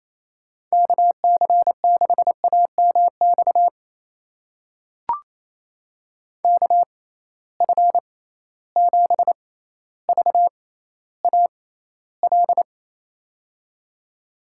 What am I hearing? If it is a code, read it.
Morse code, heard as a keyed tone, about 28 wpm, spaced as for 14 wpm, 700 Hz. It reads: KC6AMX KF7VAL
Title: USA Call Signs - 6x Character Spacing - 28wpm